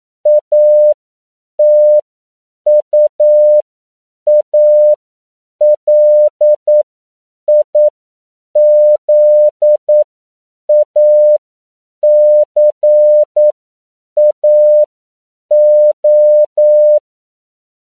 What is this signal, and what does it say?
Morse code, heard as a keyed tone, about 9 wpm, spaced as for 7 wpm, 600 Hz. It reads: ATUALIZACAO
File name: CW / 10PPM